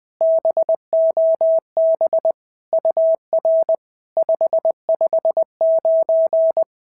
Morse code, heard as a keyed tone, about 20 wpm, 650 Hz. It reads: BOB UR 559
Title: POTA QSO Practice - 20wpm